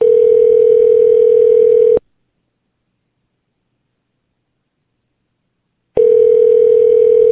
Ringback tone (North America), 440 + 480 Hz.